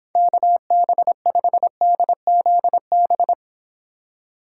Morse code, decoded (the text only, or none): KB5DZB